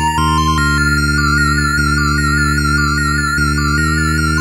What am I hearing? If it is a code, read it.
There is no code.